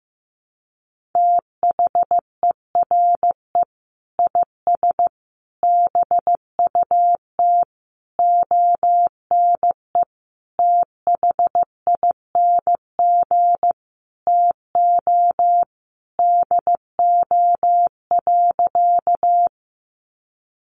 Morse code, heard as a keyed tone, about 15 wpm, 700 Hz. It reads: THERE IS BUT ONE THING TO DO.
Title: Sentences from Top 100 Words 15wpm